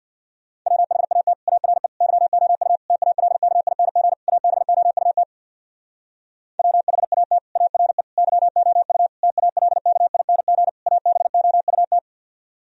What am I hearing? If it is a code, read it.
Morse code, heard as a keyed tone, about 64 wpm, 700 Hz. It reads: WHAT ARE YOU TALKING ABOUT WHAT ARE YOU TALKING ABOUT